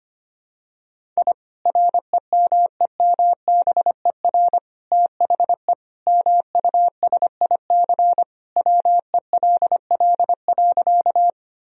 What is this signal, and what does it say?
Morse code, heard as a keyed tone, about 25 wpm, 700 Hz. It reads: I REMEMBER THE MUSIC WELL.
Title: Sentences from Top 1000 Words - Review Part 1 - 25wpm